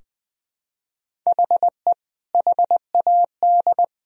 Morse code, heard as a keyed tone, about 20 wpm, 700 Hz. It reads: HE HAD
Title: Sentences from Top 1000 Words - Review Part 4 - 20wpm